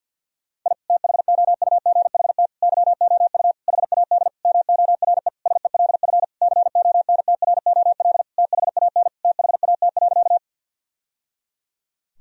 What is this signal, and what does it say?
Morse code, heard as a keyed tone, about 52 wpm, 700 Hz. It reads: I THOUGHT YOU HAD MORE SELF CONTROL THAN THAT.
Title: Sentences from Top 1000 Words - Review Part 3 - 50wpm